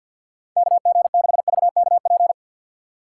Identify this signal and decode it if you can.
Morse code, heard as a keyed tone, about 53 wpm, 700 Hz. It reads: KG6VKP
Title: USA Call Signs - 6 Characters - 4x Character Spacing - 50wpm